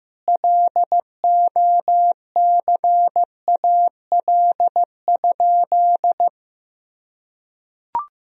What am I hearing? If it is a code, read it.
Morse code, heard as a keyed tone, about 15 wpm, 700 Hz. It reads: LOCAL?
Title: Sentences from Top 1000 Words - Review Part 3 - 15wpm